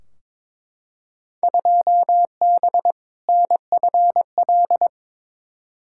Morse code, heard as a keyed tone, about 22 wpm, 700 Hz. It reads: 2B NFL
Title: Field Day Exchanges Encore - 22wpm